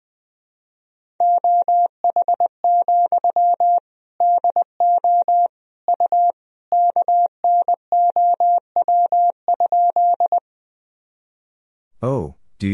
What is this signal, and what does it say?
Morse code, heard as a keyed tone, about 21 wpm, 700 Hz. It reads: OH, DO U KNOW?
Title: Sentences from Top 500 Words and Abbreviations 20wpm